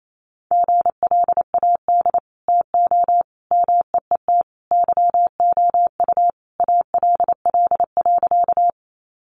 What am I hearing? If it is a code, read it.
Morse code, heard as a keyed tone, about 28 wpm, 700 Hz. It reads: GLAD TO MEET YOU ALL.
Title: Sentences from Top 1100 Words 28wpm